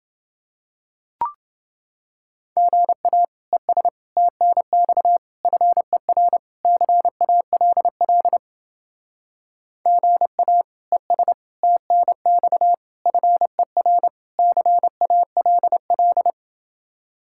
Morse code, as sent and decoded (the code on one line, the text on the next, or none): --. .- / . ... / - -. -..- / ..-. .-. / -.-. .- .-.. .-.. / --. .- / . ... / - -. -..- / ..-. .-. / -.-. .- .-.. .-..
GA ES TNX FER CALL GA ES TNX FER CALL